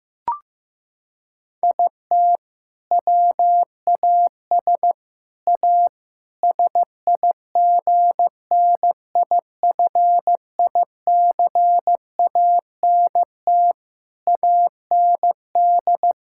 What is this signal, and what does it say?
Morse code, heard as a keyed tone, about 15 wpm, 700 Hz. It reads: IT WAS A SIGNIFICANT AND
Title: Sentences from Top 1000 Words - Review Part 6 - 15wpm